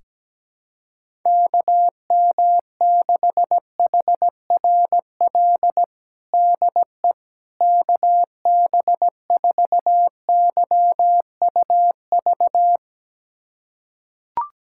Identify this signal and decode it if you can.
Morse code, heard as a keyed tone, about 17 wpm, 700 Hz. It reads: KM6HRL DE KB4YUV
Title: USA Callsign DE Callsign 17wpm